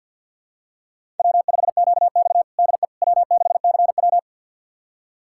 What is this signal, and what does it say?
Morse code, heard as a keyed tone, about 47 wpm, 700 Hz. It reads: W5QX DE W6CW
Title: USA Callsign DE Callsign - 4 Characters - 50wpm